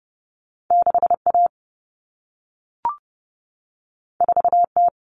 Morse code, heard as a keyed tone, about 30 wpm, 700 Hz. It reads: 6A 4T
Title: Common Cut Numbers - 2-Digit 30wpm